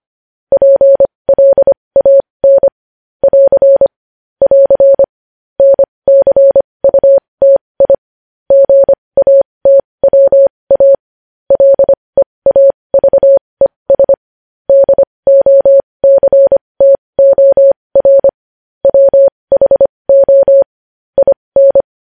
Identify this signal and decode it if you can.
Morse code, heard as a keyed tone, about 25 wpm, spaced as for 20 wpm, 550 Hz. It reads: PLAN <AR> <AR> NCUTI GATWA LEAVES DOCTOR WHO IN
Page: news-headlines-in-morse-code-at-20-wpm